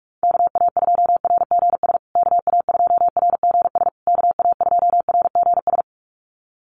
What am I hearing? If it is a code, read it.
Morse code, heard as a keyed tone, about 44 wpm, 690 Hz. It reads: KA2RGS KA2RGS KA2RGS
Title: USA Call Signs Repeated 3x 45wpm